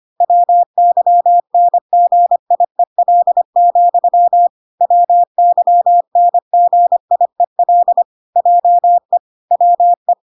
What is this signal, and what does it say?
Morse code, heard as a keyed tone, about 25 wpm, 700 Hz. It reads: WYNGIEL, WYNGIEL JE WE